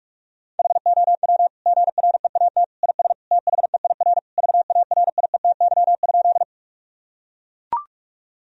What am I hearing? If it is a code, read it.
Morse code, heard as a keyed tone, about 41 wpm, 700 Hz. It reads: HOW GREAT IS THEIR VARIETY?
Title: Sentences from Top 1000 Words - Review Part 4 - 45wpm